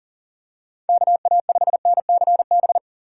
Morse code, heard as a keyed tone, about 40 wpm, 700 Hz. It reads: KA5NCB
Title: USA Call Signs - 6 Characters - 3x Character Spacing - 40wpm